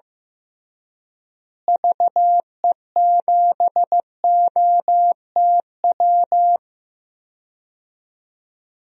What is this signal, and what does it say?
Morse code, heard as a keyed tone, about 16 wpm, 700 Hz. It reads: VE7OTW
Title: DXCC-100 Call Signs - 6 Characters - Repeated 3x - 15wpm